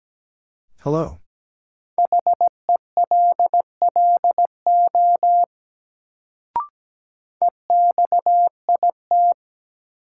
Morse code, heard as a keyed tone, about 17 wpm, 700 Hz. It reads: HELLO EXIT
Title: Sentences from Top 1100 Words - New Word Drill - 17wpm